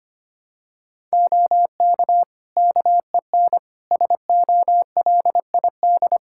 Morse code, heard as a keyed tone, about 25 wpm, 700 Hz. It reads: OK KEN SOLID
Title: Practice QSO - Exchange 3 - More Info - 25wpm